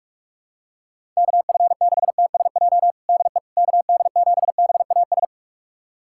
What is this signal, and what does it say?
Morse code, heard as a keyed tone, about 46 wpm, 700 Hz. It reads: KF6TSJ DE KD7BAS
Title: USA Callsign DE Callsign - 6 Characters - 45wpm